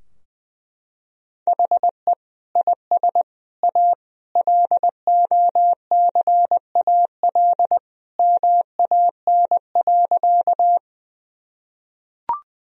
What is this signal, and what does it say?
Morse code, heard as a keyed tone, about 20 wpm, 700 Hz. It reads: HE IS A LOCAL MAN.
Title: Sentences from Top 1000 Words - Review Part 3 - 20wpm